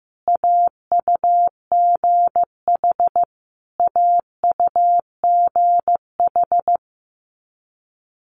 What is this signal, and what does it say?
Morse code, heard as a keyed tone, about 15 wpm, 700 Hz. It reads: AUGH AUGH
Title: Multi-letter Phonemes Repeated 3x 15wpm